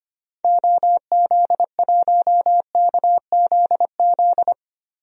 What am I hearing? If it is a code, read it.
Morse code, heard as a keyed tone, about 25 wpm, 700 Hz. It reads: OZ1KZZ
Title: DXCC-100 Call Signs - 6 Characters - Repeated 2x - 25wpm